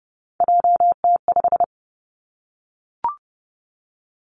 Morse code, heard as a keyed tone, about 30 wpm, 700 Hz. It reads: JT5
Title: Three Challenging Letter-Number Combinations 30wpm